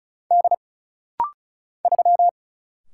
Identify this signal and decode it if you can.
Morse code, heard as a keyed tone, about 36 wpm, 700 Hz. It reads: D 3
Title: Single Letter-Number Alt - Rapid-Fire 35wpm